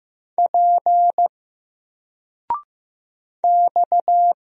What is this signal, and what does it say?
Morse code, heard as a keyed tone, about 15 wpm, 700 Hz. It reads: P X
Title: Single Letter-Number 15wpm